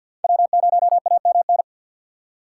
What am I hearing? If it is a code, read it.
Morse code, heard as a keyed tone, about 50 wpm, 690 Hz. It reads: W0AMN
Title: USA Call Signs - 5 Characters - 3x Character Spacing - 50wpm